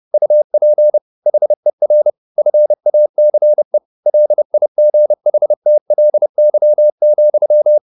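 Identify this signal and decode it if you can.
Morse code, heard as a keyed tone, about 30 wpm, 600 Hz. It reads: UP HER FACE LIGHTLY,
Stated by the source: Morse code at 30 wpm, the novel Never by Ken Follett in English, from CW med Gnister.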